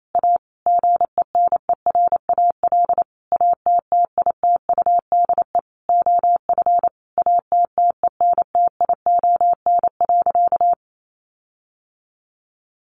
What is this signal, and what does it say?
Morse code, heard as a keyed tone, about 28 wpm, 700 Hz. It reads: A GENERAL ATTITUDE OF ATTENTION.
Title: Sentences from Top 1300 Words 28wpm